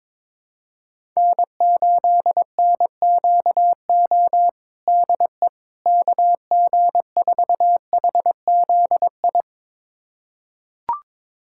Morse code, heard as a keyed tone, about 22 wpm, 700 Hz. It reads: N8NQO DE KG4HZI
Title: USA Callsign DE Callsign 22wpm